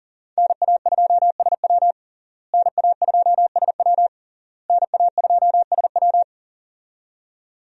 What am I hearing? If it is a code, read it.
Morse code, heard as a keyed tone, about 37 wpm, 700 Hz. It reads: NA2SW NA2SW NA2SW
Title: USA Call Signs - 5 Characters - Repeated 3x - 40wpm